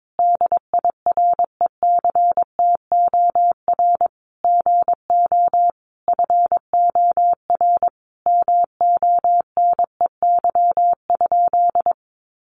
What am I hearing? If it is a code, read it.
Morse code, heard as a keyed tone, about 22 wpm, 700 Hz. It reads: DIRECTOR GO FOR MONEY?
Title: Sentences from Top 1300 Words 22wpm